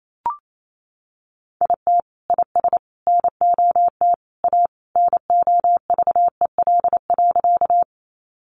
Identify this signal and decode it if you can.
Morse code, heard as a keyed tone, about 28 wpm, 700 Hz. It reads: IT IS NOT A NOVEL.
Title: Sentences from Top 1300 Words 28wpm